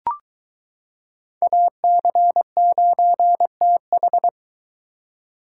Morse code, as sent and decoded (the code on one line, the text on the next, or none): .- -.-. ----. - ....
AC9TH